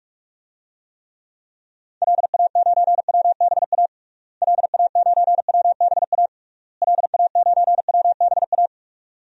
Morse code, heard as a keyed tone, about 43 wpm, 700 Hz. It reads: LA9WBA LA9WBA LA9WBA
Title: DXCC-100 Call Signs - 6 Characters - Repeated 3x - 45wpm